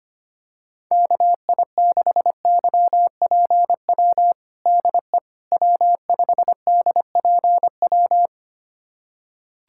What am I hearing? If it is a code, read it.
Morse code, heard as a keyed tone, about 25 wpm, 700 Hz. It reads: KI6YPW DE W5DPW
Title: USA Callsign DE Callsign 25wpm